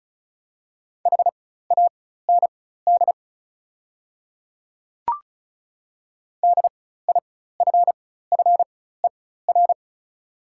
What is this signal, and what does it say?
Morse code, heard as a keyed tone, about 36 wpm, spaced as for 16 wpm, 700 Hz. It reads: HAND DIFFER
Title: 4x Character Spacing - Top 200 Words - 35wpm